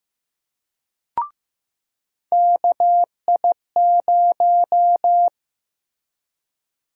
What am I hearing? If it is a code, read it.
Morse code, heard as a keyed tone, about 15 wpm, 700 Hz. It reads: KI0
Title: USA Call Sign Prefixes - Repeated 2x - 15wpm